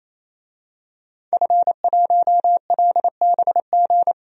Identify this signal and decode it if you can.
Morse code, heard as a keyed tone, about 28 wpm, 700 Hz. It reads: F1LBG
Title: DXCC-100 Call Signs - 5 Characters - 28wpm